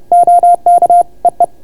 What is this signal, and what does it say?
Morse code, heard as a keyed tone, about 29 wpm, 690 Hz. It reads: OK EE